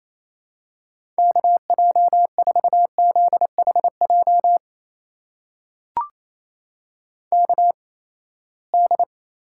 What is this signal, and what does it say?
Morse code, heard as a keyed tone, about 28 wpm, spaced as for 19 wpm, 700 Hz. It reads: KJ4ZHJ KD
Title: USA Call Signs - 6 Characters - 8x Character Spacing - 28wpm